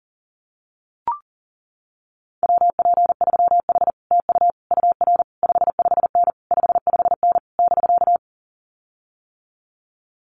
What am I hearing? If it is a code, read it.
Morse code, heard as a keyed tone, about 40 wpm, 700 Hz. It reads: WP3H TU UR 55N 55N <BK>
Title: Common QSO Phrases Encore 40wpm